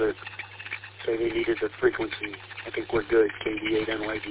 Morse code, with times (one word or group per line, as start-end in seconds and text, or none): none